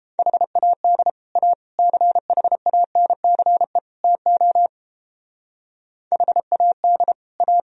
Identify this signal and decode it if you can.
Morse code, heard as a keyed tone, about 29 wpm, 700 Hz. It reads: HAD A CHANCE TO HAD A